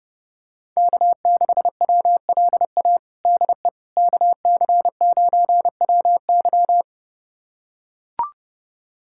Morse code, heard as a keyed tone, about 32 wpm, 700 Hz. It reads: K6WLA DE KC9WY
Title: USA Callsign DE Callsign 30wpm